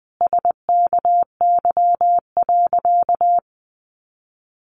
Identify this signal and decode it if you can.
Morse code, heard as a keyed tone, about 20 wpm, 700 Hz. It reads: SKY.
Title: Sentences from Top 1200 Words 20wpm